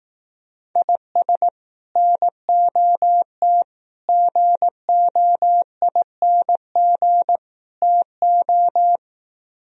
Morse code, heard as a keyed tone, about 18 wpm, 700 Hz. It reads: IS NOT GOING TO
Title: Speed-Racing - Sets of 4 Words 15wpm